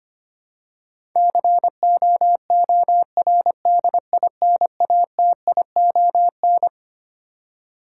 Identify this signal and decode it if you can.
Morse code, heard as a keyed tone, about 25 wpm, 700 Hz. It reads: COORDINATION
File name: Twelve Letter Words - 25wpm